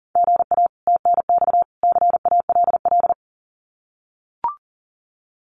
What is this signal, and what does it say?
Morse code, heard as a keyed tone, about 40 wpm, 700 Hz. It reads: GA TNX CALL
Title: Common QSO Phrases 40wpm